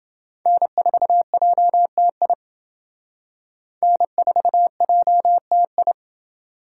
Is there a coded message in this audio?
Yes: Morse code from 0.5 to 5.9 s, heard as a keyed tone, about 28 wpm, 700 Hz. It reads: N4JTI N4JTI